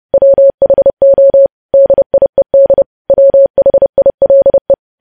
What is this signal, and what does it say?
Morse code, heard as a keyed tone, about 30 wpm, 550 Hz. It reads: WHO DIED WHILE
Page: news headlines in morse code at 30 wpm